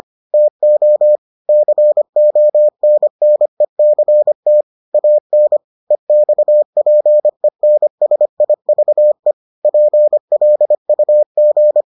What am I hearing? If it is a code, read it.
Morse code, heard as a keyed tone, about 25 wpm, 600 Hz. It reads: TO CONNECT AN EXPENSIVE PLUG